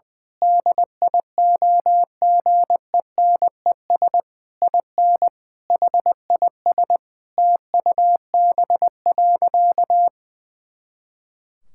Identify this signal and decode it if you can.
Morse code, heard as a keyed tone, about 19 wpm, 700 Hz. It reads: DIOGENES IN HIS TUB.
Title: Pangram Challenge v2 - 20wpm